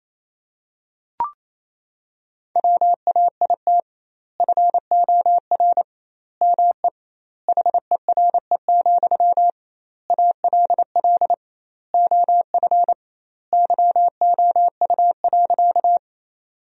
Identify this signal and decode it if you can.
Morse code, heard as a keyed tone, about 28 wpm, 700 Hz. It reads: WAIT FOR ME HERE, ALL OF YOU.
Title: 2x Word-Spacing - Sentences from Top 500 Words - 28wpm